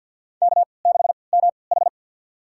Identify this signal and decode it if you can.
Morse code, heard as a keyed tone, about 48 wpm, spaced as for 28 wpm, 700 Hz. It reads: K6MH
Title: USA Call Signs - 4 Characters - 3x Character Spacing - 50wpm